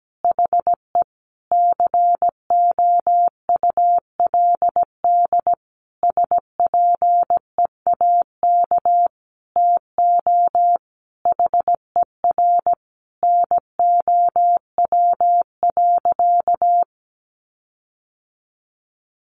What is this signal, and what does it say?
Morse code, heard as a keyed tone, about 17 wpm, 700 Hz. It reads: HE COULD SPEAK TO HER NOW.
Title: Sentences from Top 1400 Words 17wpm